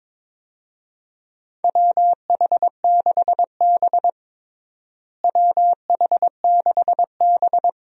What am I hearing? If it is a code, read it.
Morse code, heard as a keyed tone, about 22 wpm, 700 Hz. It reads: WH6B WH6B
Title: USA Call Signs - 4 Characters - Repeated 2x - 22wpm